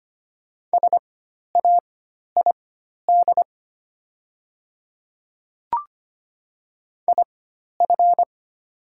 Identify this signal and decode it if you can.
Morse code, heard as a keyed tone, about 25 wpm, spaced as for 12 wpm, 690 Hz. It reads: SAID IF
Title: 4x Character Spacing - Top 100 Words - 25wpm